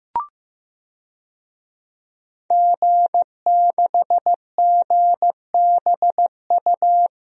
Morse code, heard as a keyed tone, about 15 wpm, 700 Hz. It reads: G6GBU
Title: DXCC-100 Call Signs - 5 Characters - Repeated 3x - 15wpm